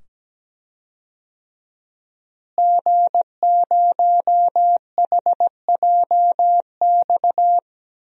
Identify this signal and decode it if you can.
Morse code, heard as a keyed tone, about 17 wpm, 700 Hz. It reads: G0HJX
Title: DXCC-100 Call Signs - 5 Characters - Repeated 2x - 17wpm